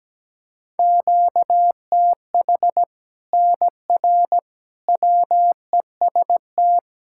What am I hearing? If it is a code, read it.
Morse code, heard as a keyed tone, about 17 wpm, 700 Hz. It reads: QTH NR WEST
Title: Practice QSO - Exchange 1 - Beginning - 17wpm